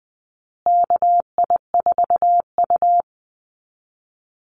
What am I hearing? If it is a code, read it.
Morse code, heard as a keyed tone, about 20 wpm, 700 Hz. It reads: KI4U
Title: USA Call Signs Repeated 3x 20wpm